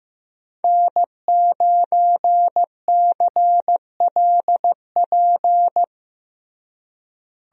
Morse code, heard as a keyed tone, about 16 wpm, 700 Hz. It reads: N9CLP
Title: USA Call Signs - 5 Characters - Repeated 2x - 15wpm